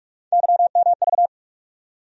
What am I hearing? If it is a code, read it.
Morse code, heard as a keyed tone, about 47 wpm, 700 Hz. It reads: YMV
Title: USA Call Sign Postfixes - 3 Letters 45wpm